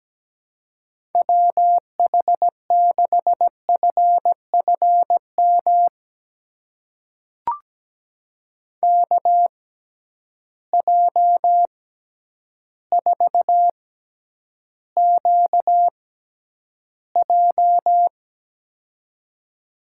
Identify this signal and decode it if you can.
Morse code, heard as a keyed tone, about 17 wpm, spaced as for 9 wpm, 700 Hz. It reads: WH6FFM KJ4QJ
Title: USA Call Signs - 6x Character Spacing - 17wpm